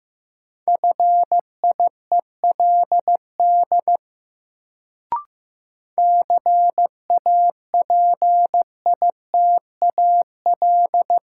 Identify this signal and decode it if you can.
Morse code, heard as a keyed tone, about 15 wpm, 700 Hz. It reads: FIELD CAPITAL
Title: Sentences from Top 1000 Words - New Word Drill - 15wpm